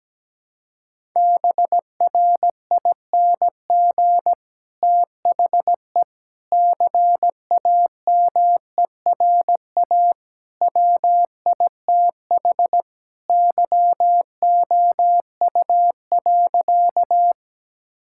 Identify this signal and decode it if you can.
Morse code, heard as a keyed tone, about 17 wpm, 700 Hz. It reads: BRING THE CAMERA WITH YOU.
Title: Sentences from Top 1000 Words - Review Part 5 - 17wpm